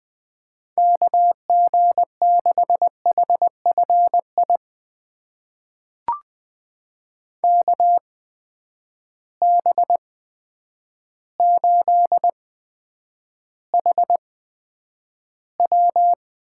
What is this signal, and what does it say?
Morse code, heard as a keyed tone, about 20 wpm, spaced as for 9 wpm, 700 Hz. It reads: KG6HFI KB8HW
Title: USA Call Signs - 6 Characters - 8x Character Spacing - 20wpm